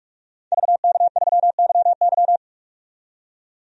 Morse code, heard as a keyed tone, about 41 wpm, 700 Hz. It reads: VK3YY
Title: DXCC-100 Call Signs - 5 Characters - 45wpm